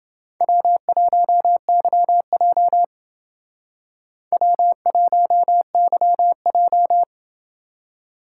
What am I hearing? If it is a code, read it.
Morse code, heard as a keyed tone, about 28 wpm, 700 Hz. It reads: W1YJ W1YJ